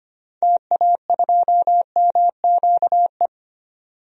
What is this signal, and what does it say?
Morse code, heard as a keyed tone, about 25 wpm, 700 Hz. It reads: TA2MQE
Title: DXCC-100 Call Signs - 6 Characters - 25wpm